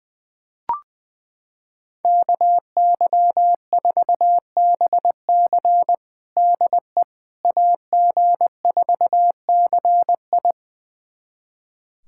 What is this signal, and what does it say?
Morse code, heard as a keyed tone, about 21 wpm, 700 Hz. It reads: KY4BC DE AG4CI
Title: USA Callsign DE Callsign - 5 Characters - 20wpm